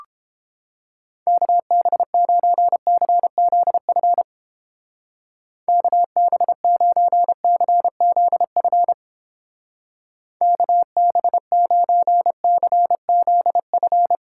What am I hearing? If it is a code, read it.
Morse code, heard as a keyed tone, about 29 wpm, 700 Hz. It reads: KB9CZF KB9CZF KB9CZF